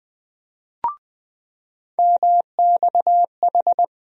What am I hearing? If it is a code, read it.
Morse code, heard as a keyed tone, about 20 wpm, 700 Hz. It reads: MXH
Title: USA Call Sign Postfixes - 3 Letters 20wpm